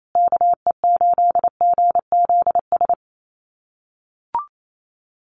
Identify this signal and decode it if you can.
Morse code, heard as a keyed tone, about 28 wpm, 700 Hz. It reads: KE8GZS
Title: USA Call Signs Repeated 3x 28wpm